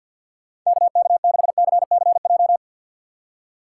Morse code, heard as a keyed tone, about 51 wpm, 690 Hz. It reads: KK6CCJ
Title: USA Call Signs - 6 Characters - 3x Character Spacing - 50wpm